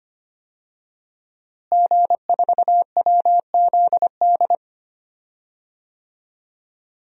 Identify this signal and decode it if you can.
Morse code, heard as a keyed tone, about 25 wpm, 700 Hz. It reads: G4WZD
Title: DXCC-100 Call Signs - 5 Characters - Repeated 2x - 25wpm